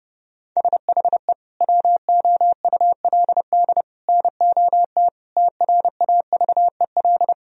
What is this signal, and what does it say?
Morse code, heard as a keyed tone, about 30 wpm, 700 Hz. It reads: SHE WOULD NOT TRAVEL
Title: Sentences from Top 1000 Words - Review Part 2 - 30wpm